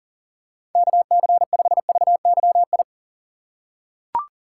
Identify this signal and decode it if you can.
Morse code, heard as a keyed tone, about 39 wpm, 700 Hz. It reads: KC5VYI
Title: USA Call Signs - 3x Character Spacing - 40wpm